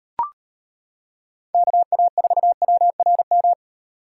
Morse code, heard as a keyed tone, about 38 wpm, 700 Hz. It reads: KA4WRM